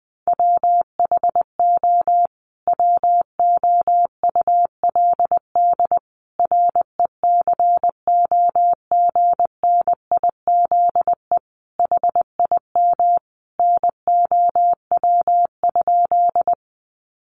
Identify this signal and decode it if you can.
Morse code, heard as a keyed tone, about 20 wpm, 700 Hz. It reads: WHO WOULD RECOGNIZE HIM NOW?
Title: Sentences from Top 1400 Words 20wpm